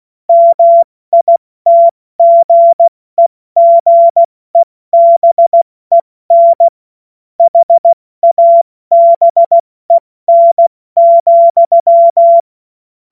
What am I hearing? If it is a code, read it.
Morse code, heard as a keyed tone, about 16 wpm, 680 Hz. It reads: MITGEGEBEN HABEN,
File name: Morsekurs / Witze / Witz 11. Juli 2023 70 F80